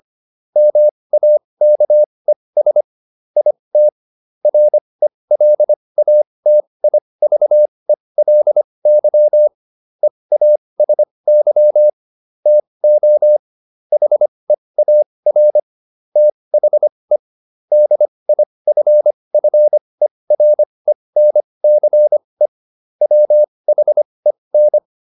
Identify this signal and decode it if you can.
Morse code, heard as a keyed tone, about 25 wpm, spaced as for 20 wpm, 600 Hz. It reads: MAKES IT RELATIVELY EASY TO HEAR THE DIFFERENCE WHEN